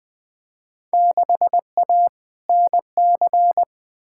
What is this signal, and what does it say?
Morse code, heard as a keyed tone, about 20 wpm, 700 Hz. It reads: 6A NC
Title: Field Day Exchanges Encore - 20wpm